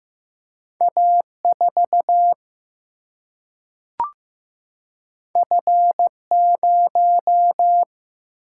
Morse code, heard as a keyed tone, about 15 wpm, 700 Hz. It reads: A 4 F 0
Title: Single-Letter Single-Number 15wpm